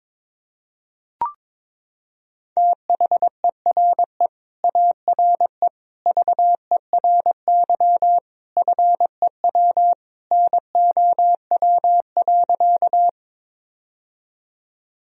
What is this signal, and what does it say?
Morse code, heard as a keyed tone, about 21 wpm, 700 Hz. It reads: THERE ARE VERY FEW NOW.
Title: Sentences from Top 1000 Words - Review Part 1 - 22wpm